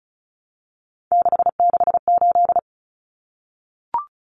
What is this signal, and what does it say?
Morse code, heard as a keyed tone, about 34 wpm, 700 Hz. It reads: 668
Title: Numbers - 3-Digit 35wpm